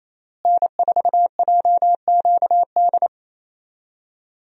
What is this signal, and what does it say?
Morse code, heard as a keyed tone, about 28 wpm, 700 Hz. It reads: N4JQD
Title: USA Call Signs - 5 Characters - Repeated 3x - 28wpm